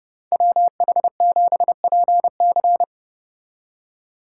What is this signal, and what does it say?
Morse code, heard as a keyed tone, about 29 wpm, 700 Hz. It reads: WH7PC